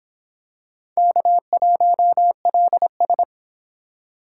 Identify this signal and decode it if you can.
Morse code, heard as a keyed tone, about 23 wpm, 700 Hz. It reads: K1LS